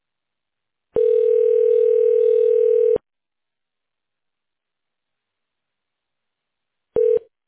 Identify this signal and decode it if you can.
Ringback tone (North America), 440 + 480 Hz.